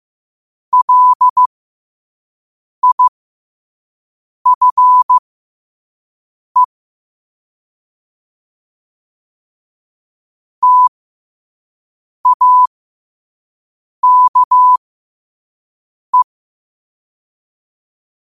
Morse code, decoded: LIFE TAKE

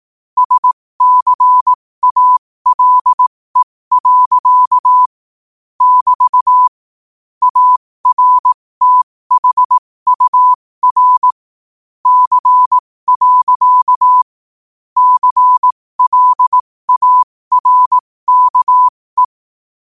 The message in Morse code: SCALE. <BT> ARTHUR C. CLARKE

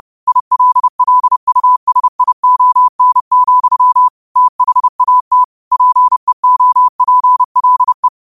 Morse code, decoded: ILLUSION, THAT PEOPLE

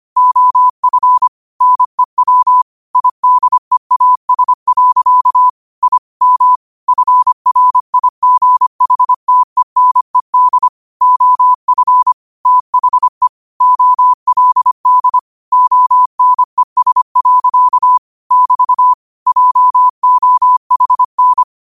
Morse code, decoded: OF NEW IDEAS. I M FRIGHTENED OF THE OLD ONES. <BT> JOHN

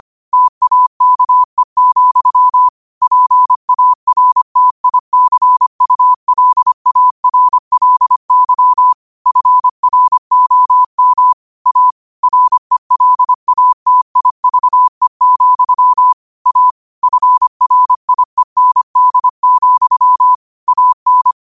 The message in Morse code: TAKE, PARTICULARLY FROM A RELATIVE, A FRIEND, AN

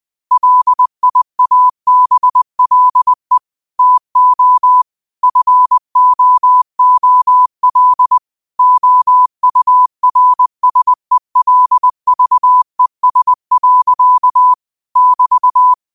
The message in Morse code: LIABLE TO FOOL OURSELVES. <BT>

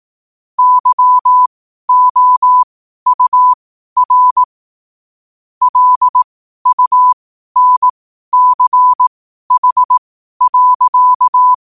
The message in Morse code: YOUR LUNCH.

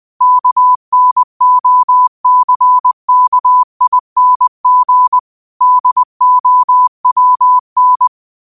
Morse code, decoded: KNOCKING DOWN

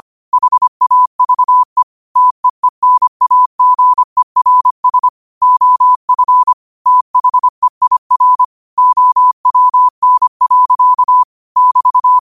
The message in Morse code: HAVE TEENAGERS OF THEIR OWN. <BT>

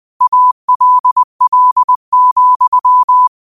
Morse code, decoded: ALL,

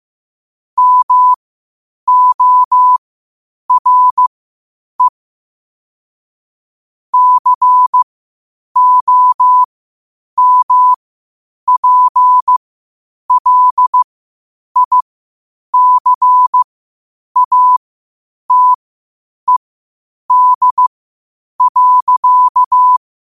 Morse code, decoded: MORE COMPLICATED.

